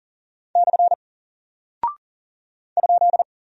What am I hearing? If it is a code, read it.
Morse code, heard as a keyed tone, about 40 wpm, 700 Hz. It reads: / ?
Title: Single Character (Letter:Number:Punctuation:Prosign) Alt 40wpm